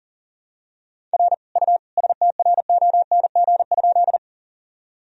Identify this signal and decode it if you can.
Morse code, heard as a keyed tone, about 40 wpm, 700 Hz. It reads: R U STRONG?
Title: Sentences from Top 500 Words and Abbreviations 40wpm